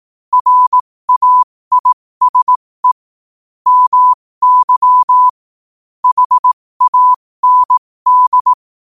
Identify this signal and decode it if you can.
Morse code, heard as a keyed tone, about 18 wpm, spaced as for 15 wpm, 1000 Hz. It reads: RAISE MY HAND